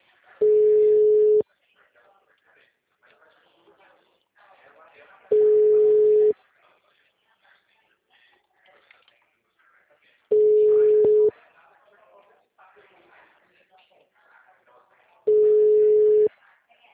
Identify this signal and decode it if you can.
Telephone call progress tone (Europe, 425 Hz), heard as a telephone network's tone, about 420 Hz, with speech. Ringback tone.